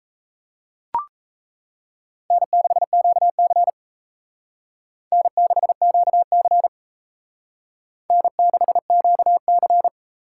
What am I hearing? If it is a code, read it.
Morse code, heard as a keyed tone, about 37 wpm, 700 Hz. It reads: N6QC N6QC N6QC